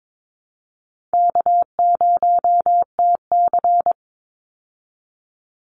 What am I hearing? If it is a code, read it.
Morse code, heard as a keyed tone, about 22 wpm, 700 Hz. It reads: K0TC
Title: Easy Call Signs Repeated 2x - 22wpm